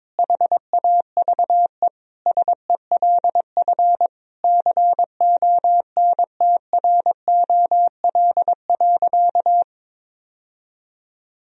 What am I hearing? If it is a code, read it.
Morse code, heard as a keyed tone, about 22 wpm, 700 Hz. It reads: HAVE SELF CONTROL.
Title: Sentences from Top 1000 Words - Review Part 3 - 22wpm